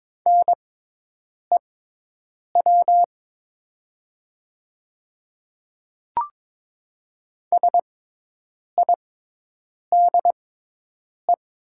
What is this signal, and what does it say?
Morse code, heard as a keyed tone, about 22 wpm, spaced as for 8 wpm, 690 Hz. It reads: NEW SIDE
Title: 6x Character Spacing - Top 500 Words - 22wpm